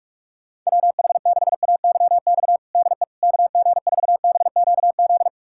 Morse code, heard as a keyed tone, about 46 wpm, 700 Hz. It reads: WH6AYX DE KG4BQZ
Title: USA Callsign DE Callsign 45wpm